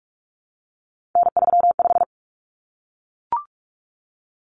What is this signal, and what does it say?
Morse code, heard as a keyed tone, about 44 wpm, 700 Hz. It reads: N35
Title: Common Cut Numbers - 3-Digit 45wpm